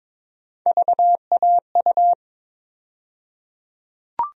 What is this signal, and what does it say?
Morse code, heard as a keyed tone, about 22 wpm, 700 Hz. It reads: VAU
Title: USA Call Sign Postfixes - 3 Letters 22wpm